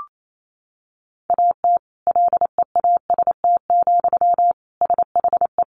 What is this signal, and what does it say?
Morse code, heard as a keyed tone, about 28 wpm, 700 Hz. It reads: AT LEAST, SHE
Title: Sentences from Top 1400 Words 28wpm